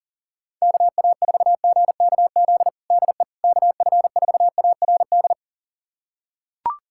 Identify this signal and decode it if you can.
Morse code, heard as a keyed tone, about 40 wpm, 700 Hz. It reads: KA4GKZ DE KF4ARD
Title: USA Callsign DE Callsign - 6 Characters - 40wpm